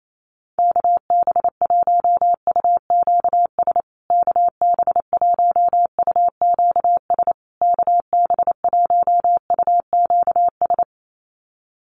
Morse code, heard as a keyed tone, about 28 wpm, 700 Hz. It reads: KB1UQS KB1UQS KB1UQS
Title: USA Call Signs Repeated 3x 28wpm